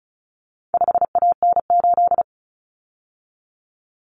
Morse code, heard as a keyed tone, about 34 wpm, 700 Hz. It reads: 5AN8
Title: Common Cut Numbers - 4-Digit 35wpm